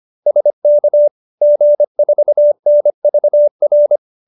Morse code, heard as a keyed tone, about 25 wpm, 590 Hz. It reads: SK G4NVR
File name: QSO315 25